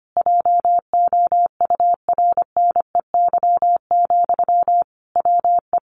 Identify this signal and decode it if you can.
Morse code, heard as a keyed tone, about 25 wpm, 700 Hz. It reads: JOURNEY, WE
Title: A Sound Of Thunder 25wpm